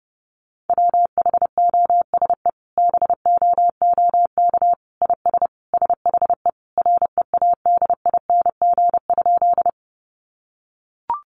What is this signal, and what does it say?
Morse code, heard as a keyed tone, about 30 wpm, 700 Hz. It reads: WHOSE BOOK IS SHE READING?
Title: Sentences from Top 1200 Words 30wpm